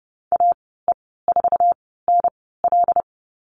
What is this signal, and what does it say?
Morse code, heard as a keyed tone, about 30 wpm, spaced as for 17 wpm, 700 Hz. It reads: AE4NL
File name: Lesson 14g - Number 2 - Call Signs 30wpm